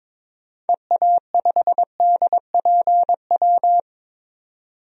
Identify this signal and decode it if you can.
Morse code, heard as a keyed tone, about 22 wpm, 700 Hz. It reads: EA5DPW
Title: DXCC-100 Call Signs - 6 Characters - 22wpm